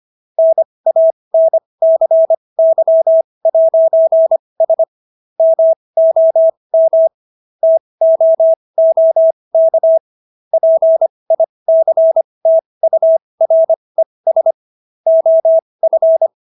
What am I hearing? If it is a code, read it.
Morse code, heard as a keyed tone, about 25 wpm, spaced as for 20 wpm, 650 Hz. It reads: NANCY'S MOM TOOK PICTURES OF